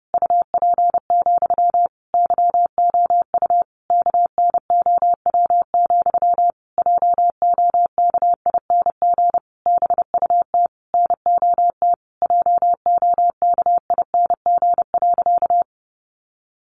Morse code, heard as a keyed tone, about 30 wpm, 700 Hz. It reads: UP, YOU KNOW, JOKING BUT NOT JOKING.
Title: A Sound Of Thunder 30wpm